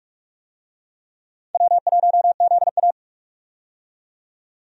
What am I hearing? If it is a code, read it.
Morse code, heard as a keyed tone, about 48 wpm, 700 Hz. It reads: W1ZA
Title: USA Call Signs - 4 Characters - 45wpm